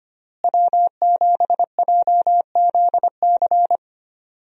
Morse code, heard as a keyed tone, about 25 wpm, 700 Hz. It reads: W7JZC
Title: USA Call Signs - 5 Characters - Repeated 2x - 25wpm